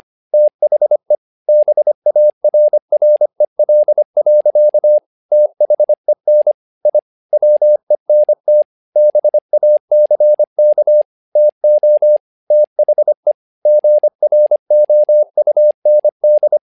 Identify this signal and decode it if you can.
Morse code, heard as a keyed tone, about 25 wpm, 600 Hz. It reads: THE BARREL. THEN I WENT BACK TO THE GROUND
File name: learning center / tower trouble / tower trouble 25wpm